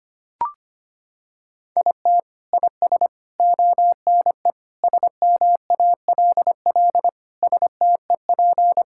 Morse code, heard as a keyed tone, about 25 wpm, 700 Hz. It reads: IT IS ONE SMALL STEP
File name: practice / 25wpm / Sentences with 9-Word Phrases - 25wpm